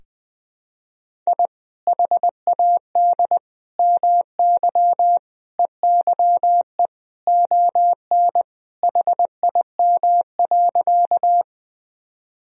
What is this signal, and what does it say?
Morse code, heard as a keyed tone, about 21 wpm, 700 Hz. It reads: I HAD MY EYE ON HIM.
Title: Sentences from Top 1000 Words - Review Part 1 - 20wpm